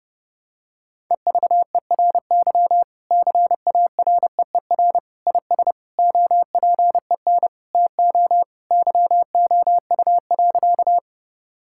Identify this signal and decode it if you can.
Morse code, heard as a keyed tone, about 32 wpm, 700 Hz. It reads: EVERY CAREER IS OPEN TO YOU.
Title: Sentences from Top 1000 Words - Review Part 6 - 30wpm